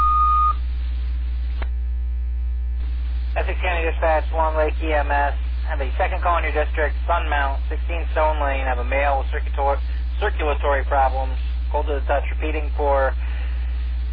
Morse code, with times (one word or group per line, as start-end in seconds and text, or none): none